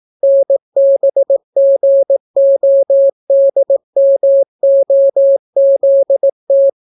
Morse code, heard as a keyed tone, about 18 wpm, 550 Hz. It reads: NBGODMOZT